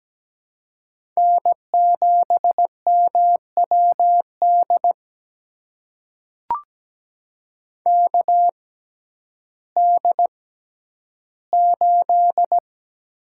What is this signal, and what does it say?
Morse code, heard as a keyed tone, about 17 wpm, spaced as for 10 wpm, 700 Hz. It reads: N7MWD KD8
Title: USA Call Signs - 6x Character Spacing - 17wpm